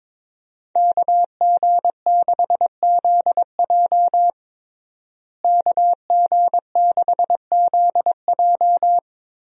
Morse code, heard as a keyed tone, about 22 wpm, 700 Hz. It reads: KG6ZJ KG6ZJ
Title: USA Call Signs - 5 Characters - Repeated 3x - 22wpm